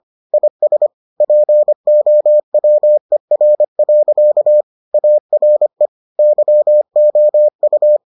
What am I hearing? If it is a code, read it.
Morse code, heard as a keyed tone, about 25 wpm, 600 Hz. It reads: IS POWER. ARE YOU